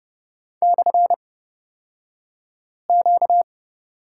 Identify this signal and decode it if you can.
Morse code, heard as a keyed tone, about 30 wpm, spaced as for 5 wpm, 700 Hz. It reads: /Q
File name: Lesson 38d - Letter q - Two Characters 30wpm